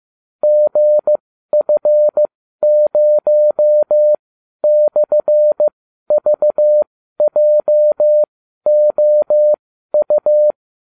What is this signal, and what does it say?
Morse code, heard as a keyed tone, about 15 wpm, spaced as for 12 wpm, 600 Hz. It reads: GF0/VJOU